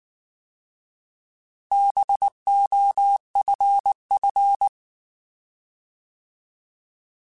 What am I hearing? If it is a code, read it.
Morse code, heard as a keyed tone, about 19 wpm, 800 Hz. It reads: BOFF
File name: Alphabetical / Alphabetical mp3s / Bs @18